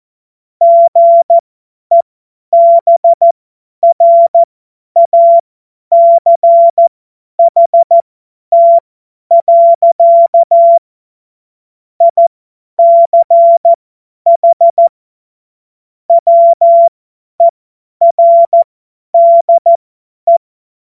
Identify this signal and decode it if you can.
Morse code, heard as a keyed tone, about 14 wpm, spaced as for 10 wpm, 680 Hz. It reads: GEBRACHT. ICH WERDE
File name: Morsekurs / Witze / Witz 06. Dezember 2022 50 F70